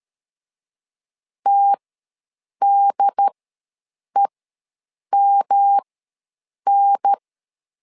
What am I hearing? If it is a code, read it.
Morse code, heard as a keyed tone, about 13 wpm, spaced as for 7 wpm, 800 Hz. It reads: TDEMN
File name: LEZIONI / 3 MWRO / 3 M ws 12wpm 1745